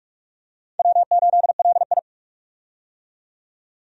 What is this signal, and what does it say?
Morse code, heard as a keyed tone, about 44 wpm, 700 Hz. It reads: W8LI
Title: USA Call Signs - 4 Characters - 45wpm